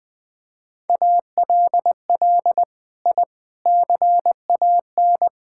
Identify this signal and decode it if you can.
Morse code, heard as a keyed tone, about 20 wpm, 700 Hz. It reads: ALL I CAN
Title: Sets of 6 Words - 20wpm